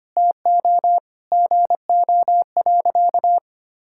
Morse code, heard as a keyed tone, about 25 wpm, 700 Hz. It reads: TO GO.
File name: practice / 25wpm / Sentences with 4-Word Phrases - 25wpm